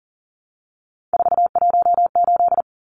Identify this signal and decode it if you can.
Morse code, heard as a keyed tone, about 40 wpm, 700 Hz. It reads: 418
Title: Numbers - 3-Digit 40wpm